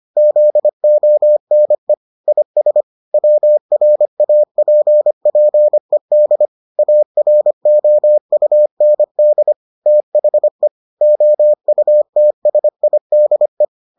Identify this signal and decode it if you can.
Morse code, heard as a keyed tone, about 25 wpm, 600 Hz. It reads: ZONE IS WRAPPED AROUND THE OUTSIDE